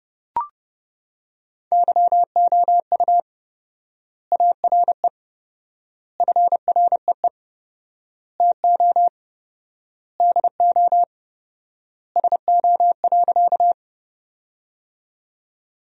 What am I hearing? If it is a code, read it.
Morse code, heard as a keyed tone, about 30 wpm, 700 Hz. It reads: YOU ARE FREE TO DO SO.
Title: 4x Word-Spacing - Sentences from Top 500 Words - 30wpm